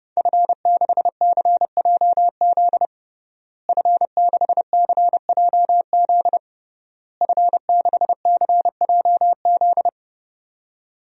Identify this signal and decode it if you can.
Morse code, heard as a keyed tone, about 30 wpm, 700 Hz. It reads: F6CJZ F6CJZ F6CJZ
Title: DXCC-100 Call Signs - 5 Characters - Repeated 3x - 30wpm